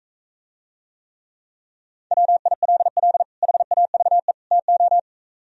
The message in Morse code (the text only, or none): WILL HAVE TO